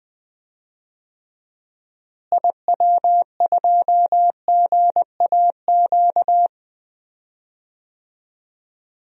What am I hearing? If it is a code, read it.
Morse code, heard as a keyed tone, about 21 wpm, 700 Hz. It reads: IW2GAQ
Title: DXCC-100 Call Signs - 6 Characters - Repeated 3x - 20wpm